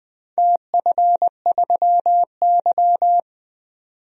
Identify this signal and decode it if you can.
Morse code, heard as a keyed tone, about 20 wpm, 700 Hz. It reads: TF3Y
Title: DXCC-100 Call Signs - 4 Characters - 20wpm